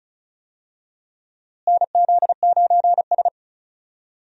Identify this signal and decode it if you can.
Morse code, heard as a keyed tone, about 36 wpm, 700 Hz. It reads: NZ9S
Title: USA Call Signs - 4 Characters - Repeated 3x - 35wpm